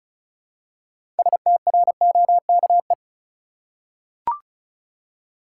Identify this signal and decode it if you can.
Morse code, heard as a keyed tone, about 36 wpm, 700 Hz. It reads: STROKE
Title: Six Letter Words - 35wpm